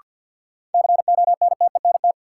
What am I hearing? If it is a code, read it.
Morse code, heard as a keyed tone, about 53 wpm, 700 Hz. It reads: CONTENT
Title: Sentences from Top 900 Words - New Word Drill - 50wpm